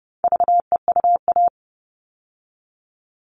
Morse code, heard as a keyed tone, about 28 wpm, 700 Hz. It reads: VEUA
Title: All Cut Numbers - 4-Digit 30wpm